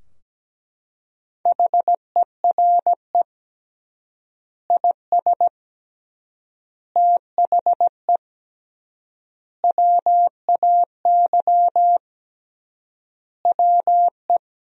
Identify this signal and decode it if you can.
Morse code, heard as a keyed tone, about 17 wpm, 700 Hz. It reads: HERE IS THE WAY WE
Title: 3x Word-Spacing - Sentences from Top 200 Words - 17wpm